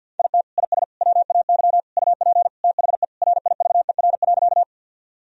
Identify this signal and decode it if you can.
Morse code, heard as a keyed tone, about 50 wpm, 700 Hz. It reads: IT IS WAY UP THE RIVER.
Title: Sentences from Top 1000 Words - Review Part 1 - 50wpm